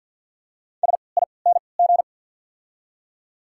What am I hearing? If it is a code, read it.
Morse code, heard as a keyed tone, about 48 wpm, spaced as for 28 wpm, 690 Hz. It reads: SING
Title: 3x Character Spacing - Top 500 Words - 50wpm